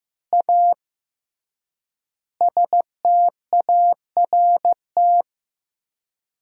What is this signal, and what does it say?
Morse code, heard as a keyed tone, about 15 wpm, 700 Hz. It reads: A START
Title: 3x Word-Spacing - Sentences from Top 300 Words - 15wpm